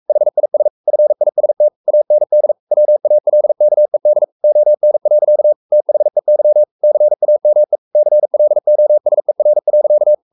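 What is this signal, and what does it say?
Morse code, heard as a keyed tone, about 44 wpm, 600 Hz. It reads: HIS FIST AND WALKED ON. THEY CAME CLOSER.
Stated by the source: Morse code at 43 wpm sending Ken Follett's novel Never in English, from CW med Gnister.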